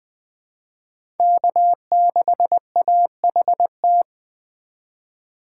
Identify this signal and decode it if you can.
Morse code, heard as a keyed tone, about 21 wpm, 700 Hz. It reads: K6AHT
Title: USA Call Signs - 5 Characters - 4x Character Spacing - 20wpm